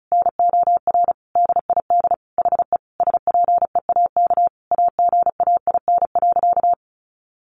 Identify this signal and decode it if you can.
Morse code, heard as a keyed tone, about 34 wpm, 700 Hz. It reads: NOR DID HE SPEAK AGAIN.
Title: Sentences from Top 1500 Words 35wpm